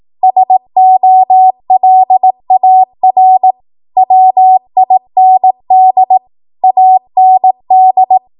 Morse code, heard as a keyed tone, about 18 wpm, 750 Hz. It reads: SOLAR WIND AND
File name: Morse / 18WPM